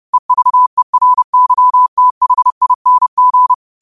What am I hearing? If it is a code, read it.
Morse code, heard as a keyed tone, about 30 wpm, 1000 Hz. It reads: EVERYTHING
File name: qotdfiles / 30wpm